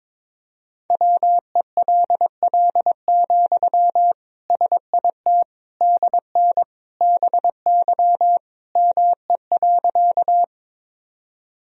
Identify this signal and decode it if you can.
Morse code, heard as a keyed tone, about 22 wpm, 700 Hz. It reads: WELL, SIT DN BY ME.
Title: Sentences from Top 500 Words and Abbreviations 22wpm